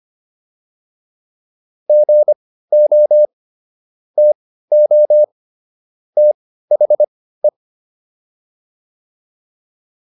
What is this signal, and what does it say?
Morse code, heard as a keyed tone, about 25 wpm, spaced as for 15 wpm, 600 Hz. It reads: GO TO THE